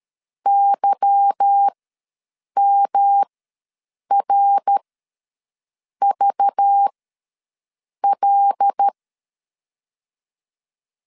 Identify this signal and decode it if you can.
Morse code, heard as a keyed tone, about 13 wpm, spaced as for 6 wpm, 800 Hz. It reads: YMRVL
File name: LEZIONI / 8 YJ45 / 8 Y ws 12wpm 3409